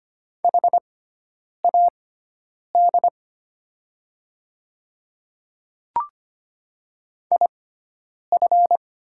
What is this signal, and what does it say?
Morse code, heard as a keyed tone, about 25 wpm, spaced as for 9 wpm, 700 Hz. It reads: HAD IF
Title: 6x Character Spacing - Top 200 Words - 25wpm